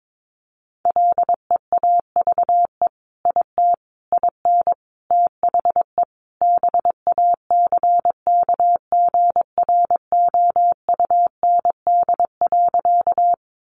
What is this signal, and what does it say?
Morse code, heard as a keyed tone, about 22 wpm, 700 Hz. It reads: LEAVE IT IN THE BACKGROUND.
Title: Sentences from Top 1300 Words 22wpm